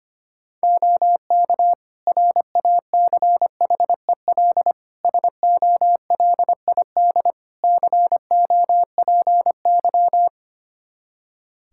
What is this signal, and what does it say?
Morse code, heard as a keyed tone, about 25 wpm, 700 Hz. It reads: OK RACHEL SOLID COPY
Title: K1USN SST QSO Practice 25wpm